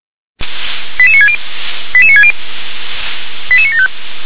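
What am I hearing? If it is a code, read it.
Five-tone selective call (ZVEI), heard as quick runs of five tones, tones of about 70 ms. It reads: 88960 88960 88975
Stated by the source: ZVEI five-tone selective call received by SDR.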